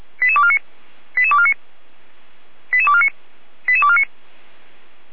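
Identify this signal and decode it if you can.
Five-tone selective call (ZVEI), heard as quick runs of five tones, tones of about 70 ms. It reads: 88249 88249 88249 88249
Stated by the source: ZVEI five-tone selective call 88209 received with an SDR.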